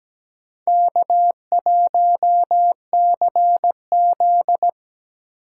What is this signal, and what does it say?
Morse code, heard as a keyed tone, about 17 wpm, 700 Hz. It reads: K1CZ
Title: USA Call Signs - 4 Characters - 3x Character Spacing - 17wpm